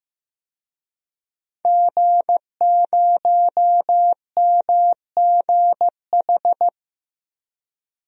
Morse code, heard as a keyed tone, about 15 wpm, 700 Hz. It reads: G0MGH
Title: DXCC-100 Call Signs - 5 Characters - Repeated 2x - 15wpm